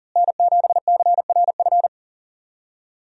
Morse code, heard as a keyed tone, about 39 wpm, 700 Hz. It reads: N7CRF
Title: USA Call Signs - 5 Characters - Repeated 3x - 40wpm